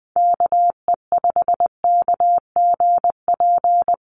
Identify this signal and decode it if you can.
Morse code, heard as a keyed tone, about 19 wpm, 700 Hz. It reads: KE5KGP
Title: USA Call Signs Repeated 3x 20wpm